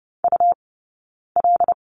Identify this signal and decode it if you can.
Morse code, heard as a keyed tone, about 30 wpm, spaced as for 9 wpm, 690 Hz. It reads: UL
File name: Lesson 17d - Letter u - Two Characters 30wpm